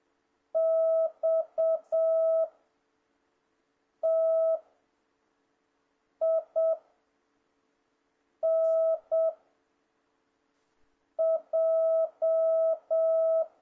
Morse code, heard as a keyed tone, about 7 wpm, 650 Hz. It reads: X T I N J
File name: jh8cbh / 2026cw